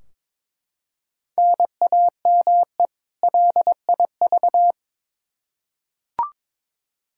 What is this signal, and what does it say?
Morse code, heard as a keyed tone, about 21 wpm, 700 Hz. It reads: NAME LIV
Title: Practice QSO - Exchange 1 - Beginning - 22wpm